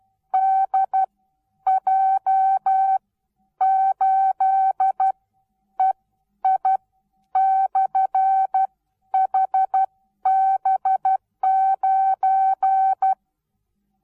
Morse code, heard as a keyed tone, about 12 wpm, spaced as for 9 wpm, 770 Hz. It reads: DJ8EI/HB9
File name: Onlinelehrgang / morsen / DJ4UF Morsen19